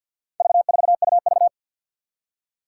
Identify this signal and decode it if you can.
Morse code, heard as a keyed tone, about 52 wpm, 700 Hz. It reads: V4UV
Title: Four Challenging Letter-Number Combinations 50wpm